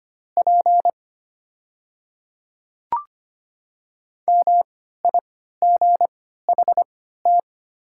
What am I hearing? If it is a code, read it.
Morse code, heard as a keyed tone, about 25 wpm, spaced as for 14 wpm, 700 Hz. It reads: P MIGHT
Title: 3x Character Spacing - Top 300 Words - 25wpm